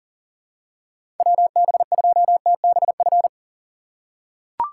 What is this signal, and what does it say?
Morse code, heard as a keyed tone, about 40 wpm, 700 Hz. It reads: WB2TBF
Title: USA Call Signs - 6 Characters - 6x Character Spacing - 40wpm